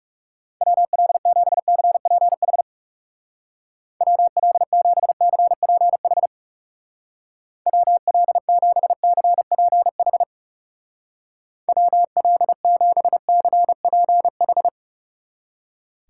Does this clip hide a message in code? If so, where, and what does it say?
Yes: Morse code from 0.6 to 14.7 s, heard as a keyed tone, about 38 wpm, 700 Hz. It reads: WL7CPH WL7CPH WL7CPH WL7CPH